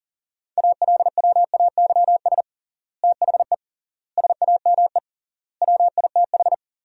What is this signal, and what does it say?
Morse code, heard as a keyed tone, about 40 wpm, 700 Hz. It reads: ALWAYS THE SAME WITH
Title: 3x Word-Spacing - Sentences from Top 300 Words - 40wpm